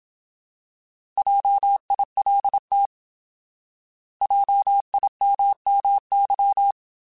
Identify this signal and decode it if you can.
Morse code, heard as a keyed tone, about 27 wpm, 800 Hz. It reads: JILT JIMMY
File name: Alphabetical / Alphabetical mp3s / Js @25